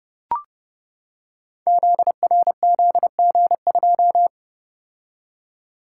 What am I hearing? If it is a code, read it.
Morse code, heard as a keyed tone, about 30 wpm, 700 Hz. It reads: ZRZG2
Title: Code Groups - 5 Letter:Numbers - 30wpm